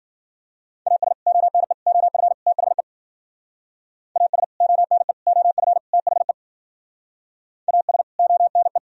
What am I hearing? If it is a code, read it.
Morse code, heard as a keyed tone, about 52 wpm, 700 Hz. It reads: AS ONE OF THE AS ONE OF THE AS ONE